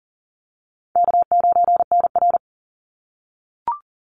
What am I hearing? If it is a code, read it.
Morse code, heard as a keyed tone, about 38 wpm, 700 Hz. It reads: K9NR
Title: Easy Call Signs - 40wpm